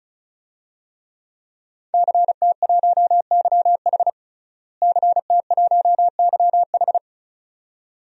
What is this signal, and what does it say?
Morse code, heard as a keyed tone, about 36 wpm, 700 Hz. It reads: CT1YH CT1YH
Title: DXCC-100 Call Signs - 5 Characters - Repeated 2x - 35wpm